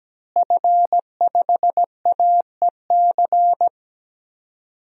Morse code, heard as a keyed tone, about 17 wpm, 700 Hz. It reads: F5AEC
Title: DXCC-100 Call Signs - 5 Characters - Repeated 3x - 17wpm